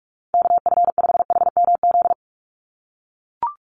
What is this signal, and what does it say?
Morse code, heard as a keyed tone, about 44 wpm, 700 Hz. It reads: KF5HMZ
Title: USA Call Signs 45wpm